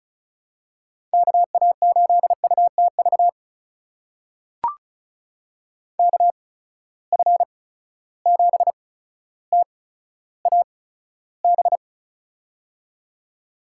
Morse code, heard as a keyed tone, about 36 wpm, spaced as for 15 wpm, 700 Hz. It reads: KA8UTV KF7TAB
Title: USA Call Signs - 6 Characters - 8x Character Spacing - 35wpm